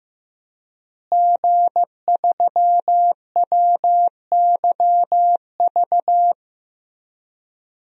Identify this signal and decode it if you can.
Morse code, heard as a keyed tone, about 15 wpm, 700 Hz. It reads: G3WYV
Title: DXCC-100 Call Signs - 5 Characters - Repeated 3x - 15wpm